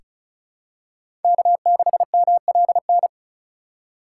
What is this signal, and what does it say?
Morse code, heard as a keyed tone, about 34 wpm, 700 Hz. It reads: K6MLN
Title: USA Call Signs - 5 Characters - 4x Character Spacing - 35wpm